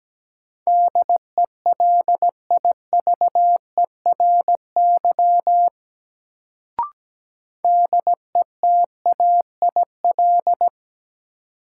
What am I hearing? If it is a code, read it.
Morse code, heard as a keyed tone, about 17 wpm, 700 Hz. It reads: DELIVERY DETAIL
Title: Sentences from Top 1500 Words - New Word Drill Alt - 17wpm